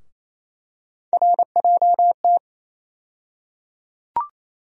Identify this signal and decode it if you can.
Morse code, heard as a keyed tone, about 28 wpm, 700 Hz. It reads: RJT
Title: USA Call Sign Postfixes - 3 Letters 28wpm